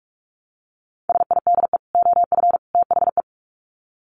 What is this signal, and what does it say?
Morse code, heard as a keyed tone, about 44 wpm, 700 Hz. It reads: SIDE OF THE
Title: Sets of 3 Words 45wpm